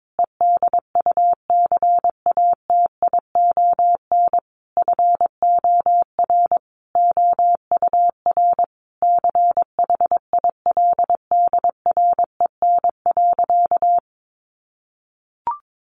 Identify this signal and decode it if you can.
Morse code, heard as a keyed tone, about 22 wpm, 700 Hz. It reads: EDUCATION FOR OUR CHILDREN.
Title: Sentences from Top 1400 Words 22wpm